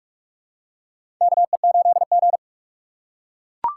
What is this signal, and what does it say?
Morse code, heard as a keyed tone, about 44 wpm, 700 Hz. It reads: KE8G
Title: USA Call Signs - 4 Characters - 6x Character Spacing - 45wpm